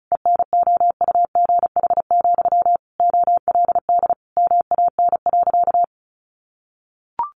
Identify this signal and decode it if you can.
Morse code, heard as a keyed tone, about 34 wpm, 700 Hz. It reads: ENOUGH, OLD MAN.
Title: Sentences from Top 1200 Words 35wpm